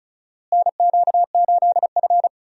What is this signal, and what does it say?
Morse code, heard as a keyed tone, about 34 wpm, 700 Hz. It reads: NQ8F
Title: USA Call Signs - 4 Characters - Repeated 3x - 35wpm